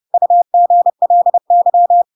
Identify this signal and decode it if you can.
Morse code, heard as a keyed tone, about 30 wpm, 700 Hz. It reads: UGLY
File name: Pinocchio / Pinocchio ch4 30 30 ik1hge